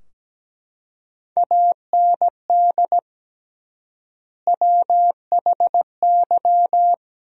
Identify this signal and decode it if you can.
Morse code, heard as a keyed tone, about 17 wpm, 700 Hz. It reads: AND WHY
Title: 3x Word-Spacing - Sentences from Top 400 Words - 17wpm